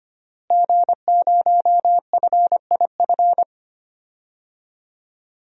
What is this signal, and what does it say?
Morse code, heard as a keyed tone, about 25 wpm, 700 Hz. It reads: G0FIF
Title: DXCC-100 Call Signs - 5 Characters - Repeated 3x - 25wpm